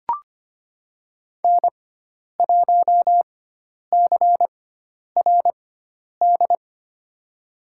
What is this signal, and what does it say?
Morse code, heard as a keyed tone, about 25 wpm, spaced as for 10 wpm, 700 Hz. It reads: N1CRD
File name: Lesson 16g - Letter c - Call Signs 25wpm